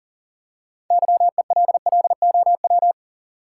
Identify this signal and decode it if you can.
Morse code, heard as a keyed tone, about 40 wpm, 700 Hz. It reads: YELLOW
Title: Six Letter Words - 40wpm